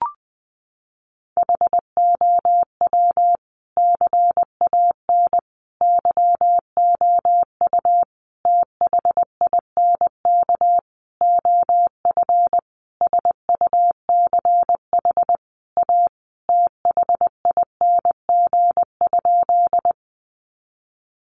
Morse code, heard as a keyed tone, about 20 wpm, 700 Hz. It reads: HOW CAN YOU THINK OF SUCH A THING?
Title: Sentences from Top 200 Words 20wpm